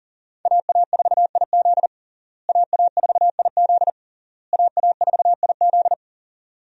Morse code, heard as a keyed tone, about 40 wpm, 700 Hz. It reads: AA4IZ AA4IZ AA4IZ
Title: USA Call Signs - 5 Characters - Repeated 3x - 40wpm